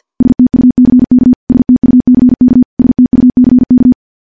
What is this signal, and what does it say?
Morse code, heard as a keyed tone, about 50 wpm, 250 Hz. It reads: STUCK STUCK STUCK
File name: morse / morse-training-50WPM